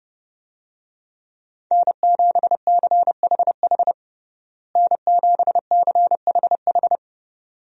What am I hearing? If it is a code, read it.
Morse code, heard as a keyed tone, about 30 wpm, 700 Hz. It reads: N7CHH N7CHH
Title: USA Call Signs - 5 Characters - Repeated 2x - 30wpm